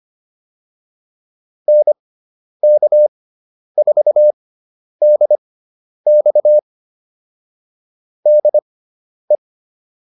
Morse code, heard as a keyed tone, about 25 wpm, spaced as for 10 wpm, 600 Hz. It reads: NK4DX DE